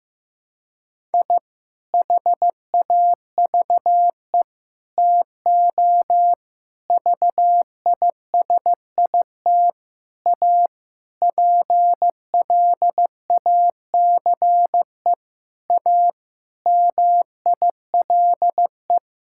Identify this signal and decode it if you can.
Morse code, heard as a keyed tone, about 15 wpm, 700 Hz. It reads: I HAVE TO VISIT A PLACE A MILE
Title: Sentences from Top 1000 Words - Review Part 6 - 15wpm